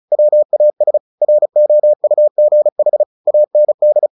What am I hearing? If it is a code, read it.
Morse code, heard as a keyed tone, about 35 wpm, 600 Hz. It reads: WAS ROUGH AND
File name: the old man and the sea / the old man and the sea 1 35wpm